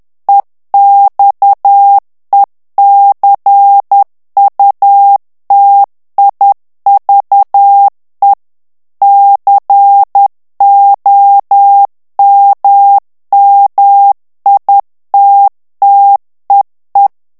Morse code, heard as a keyed tone, about 10 wpm, 800 Hz. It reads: EXECUTIVE COMMITTEE
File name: ANGLAIS texte clair 3 vitesse 10 mots minute